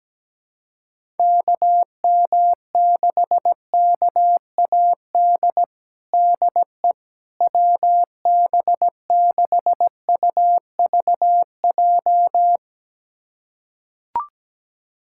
Morse code, heard as a keyed tone, about 17 wpm, 700 Hz. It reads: KM6KAD DE WB6UVJ
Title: USA Callsign DE Callsign - 6 Characters - 17wpm